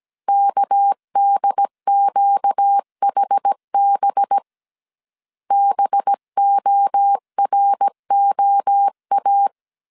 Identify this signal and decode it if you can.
Morse code, heard as a keyed tone, about 17 wpm, 800 Hz. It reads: KDQHB BOROA
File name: 16wpm